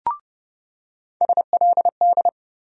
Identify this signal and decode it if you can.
Morse code, heard as a keyed tone, about 30 wpm, 700 Hz. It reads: SLD
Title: USA Call Sign Postfixes - 3 Letters 30wpm